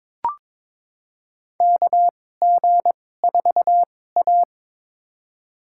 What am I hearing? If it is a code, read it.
Morse code, heard as a keyed tone, about 22 wpm, spaced as for 16 wpm, 700 Hz. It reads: KG4A
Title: USA Call Signs - 4 Characters - 2x Character Spacing - 22wpm